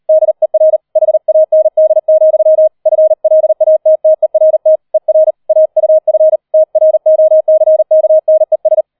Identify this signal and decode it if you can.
Morse code, heard as a keyed tone, about 38 wpm, 630 Hz. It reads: DER HAND, FLATTERT ER AUF TROCKNES